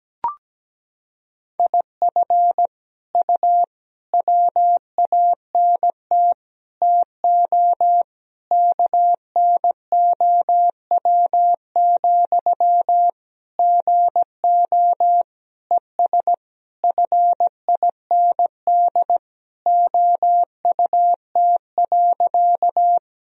Morse code, heard as a keyed tone, about 17 wpm, 700 Hz. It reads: IF U WANT TO KNOW, GO ES FIND OUT.
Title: Sentences from Top 200 Words and Abbreviations 17wpm